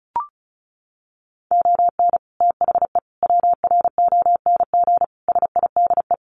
Morse code, heard as a keyed tone, about 34 wpm, 700 Hz. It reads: ON THE WRONG SIDE
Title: Sets of 5 Words Encore 35wpm